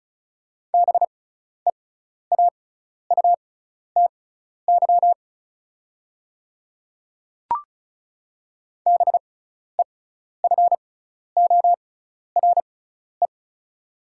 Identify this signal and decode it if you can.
Morse code, heard as a keyed tone, about 36 wpm, spaced as for 12 wpm, 690 Hz. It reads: BEAUTY BEFORE
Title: 6x Character Spacing - Top 500 Words - 35wpm